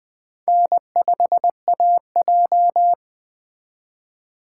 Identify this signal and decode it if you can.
Morse code, heard as a keyed tone, about 20 wpm, 700 Hz. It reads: N5AJ
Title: Speed Racing - USA 1x2 Call Signs 17wpm